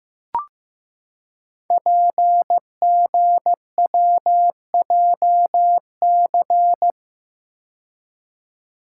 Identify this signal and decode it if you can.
Morse code, heard as a keyed tone, about 15 wpm, 700 Hz. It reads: PGWJC
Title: Code Groups - 5 Letters - 15wpm